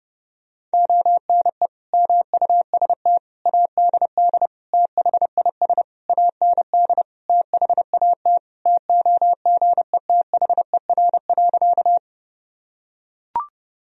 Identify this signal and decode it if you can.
Morse code, heard as a keyed tone, about 30 wpm, 700 Hz. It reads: ONE MUST ADD THIS AND THAT TOGETHER.
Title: Sentences from Top 1000 Words - Review Part 1 - 30wpm